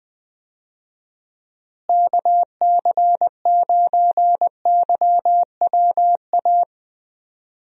Morse code, heard as a keyed tone, about 20 wpm, 700 Hz. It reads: KC9YWA
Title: USA Call Signs - 6 Characters - Repeated 2x - 20wpm